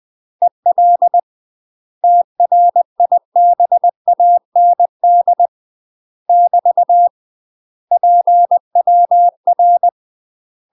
Morse code, heard as a keyed tone, about 20 wpm, 700 Hz. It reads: EL TRIBAND <BT> PWR
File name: UK MASTER QSOs TRAINING / 20WPM / UK MASTER QSOs 0052